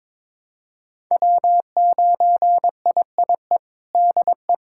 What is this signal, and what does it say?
Morse code, heard as a keyed tone, about 21 wpm, 700 Hz. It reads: W9IIE DE
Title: USA Callsign DE Callsign - 5 Characters - 22wpm